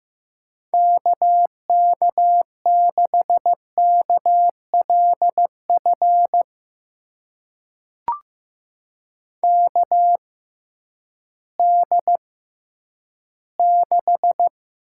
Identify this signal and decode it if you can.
Morse code, heard as a keyed tone, about 15 wpm, spaced as for 10 wpm, 700 Hz. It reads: KK6KLF KD6
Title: USA Call Signs - 6 Characters - 6x Character Spacing - 15wpm